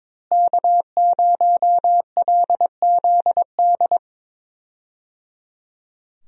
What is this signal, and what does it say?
Morse code, heard as a keyed tone, about 22 wpm, 700 Hz. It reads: K0LZD
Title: USA Call Signs - 5 Characters - Repeated 3x - 22wpm